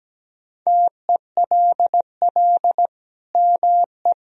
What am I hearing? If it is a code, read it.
Morse code, heard as a keyed tone, about 17 wpm, 700 Hz. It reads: TELL ME
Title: Sentences from Top 200 Words and Abbreviations 17wpm